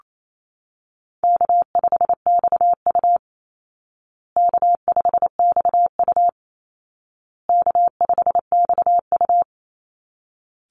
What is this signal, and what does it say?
Morse code, heard as a keyed tone, about 28 wpm, 700 Hz. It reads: K5XU K5XU K5XU
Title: Easy Call Signs Repeated 3x - 28wpm